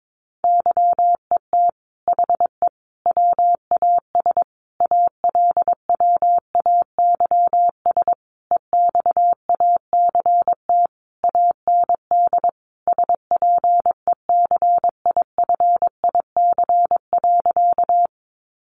Morse code, heard as a keyed tone, about 22 wpm, 700 Hz. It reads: YET HE WAS ALWAYS EXACT AND SPECIFIC.
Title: Sentences from Top 1300 Words 22wpm